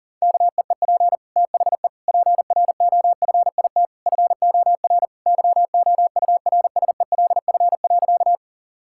Morse code, heard as a keyed tone, about 40 wpm, 700 Hz. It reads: KEEP THE PROFIT FOR YOURSELF.
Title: Sentences from Top 1000 Words - Review Part 5 - 40wpm